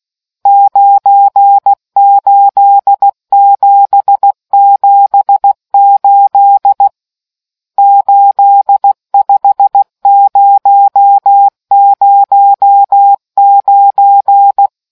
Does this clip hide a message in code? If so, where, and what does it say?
Yes: Morse code from 0.5 to 14.7 s, heard as a keyed tone, about 16 wpm, 800 Hz. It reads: 98778 85009